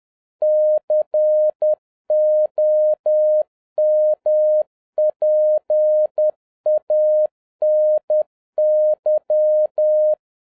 Morse code, heard as a keyed tone, about 10 wpm, 610 Hz. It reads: COMPANY